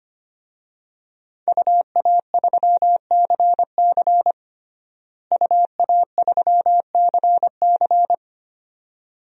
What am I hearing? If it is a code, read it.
Morse code, heard as a keyed tone, about 25 wpm, 700 Hz. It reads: UA3CC UA3CC
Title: DXCC-100 Call Signs - 5 Characters - Repeated 2x - 25wpm